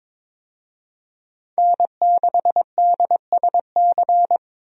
Morse code, heard as a keyed tone, about 21 wpm, 700 Hz. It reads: N6DSC
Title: USA Call Signs - 5 Characters - Repeated 2x - 22wpm